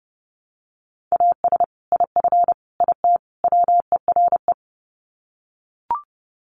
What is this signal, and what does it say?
Morse code, heard as a keyed tone, about 30 wpm, 700 Hz. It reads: AS IF IT WERE
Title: Sets of 4 Words 30wpm